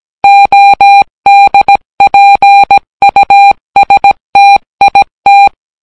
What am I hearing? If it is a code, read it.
Morse code, heard as a keyed tone, about 17 wpm, 800 Hz. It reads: ODPUSTIT